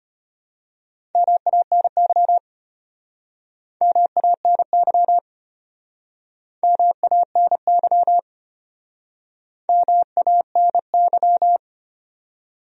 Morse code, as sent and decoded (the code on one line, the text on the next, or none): -- .- -. -.-- / -- .- -. -.-- / -- .- -. -.-- / -- .- -. -.--
MANY MANY MANY MANY